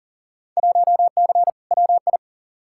Morse code, heard as a keyed tone, about 40 wpm, 700 Hz. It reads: 1C WI
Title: Field Day Exchanges - 40wpm